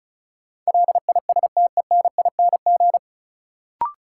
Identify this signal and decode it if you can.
Morse code, heard as a keyed tone, about 36 wpm, 700 Hz. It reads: LISTENING
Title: Sets of 5 Words - New Word Drill - 35wpm